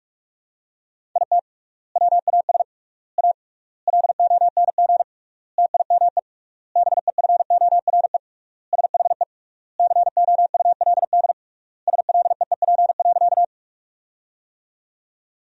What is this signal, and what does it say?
Morse code, heard as a keyed tone, about 46 wpm, 700 Hz. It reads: IT WAS A LONG TIME BEFORE SHE COULD SLEEP.
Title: 3x Word-Spacing - Sentences from Top 500 Words - 45wpm